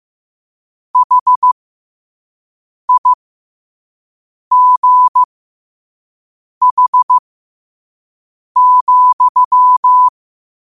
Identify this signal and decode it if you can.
Morse code, heard as a keyed tone, about 15 wpm, spaced as for 5 wpm, 1000 Hz. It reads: HIGH,
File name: qotdfiles / 5wpm